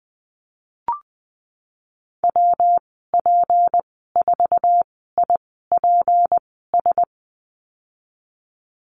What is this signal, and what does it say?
Morse code, heard as a keyed tone, about 20 wpm, spaced as for 15 wpm, 700 Hz. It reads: WP4IPS
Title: 231 Lesson 37g - Letter x - Call Signs 20wpm